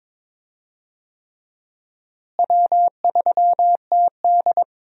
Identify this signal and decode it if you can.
Morse code, heard as a keyed tone, about 22 wpm, 700 Hz. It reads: W3TD
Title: USA Call Signs - 4 Characters - Repeated 3x - 22wpm